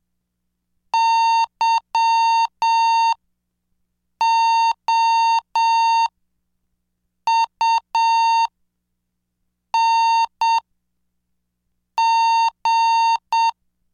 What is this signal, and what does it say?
Morse code, heard as a keyed tone, about 7 wpm, spaced as for 5 wpm, 910 Hz. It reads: YOUNG
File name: cursos / CDestudo / 20 Track 20